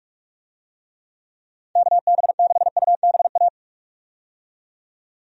Morse code, heard as a keyed tone, about 46 wpm, 700 Hz. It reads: KB6UBA
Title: USA Call Signs - 6 Characters - 45wpm